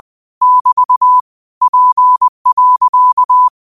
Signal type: Morse code